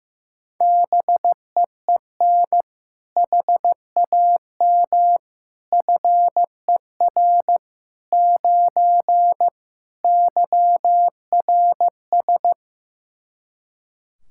Morse code, heard as a keyed tone, about 16 wpm, 700 Hz. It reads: BEEN HAM FER 9 YRS
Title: Practice QSO - Exchange 3 - More Info - 15wpm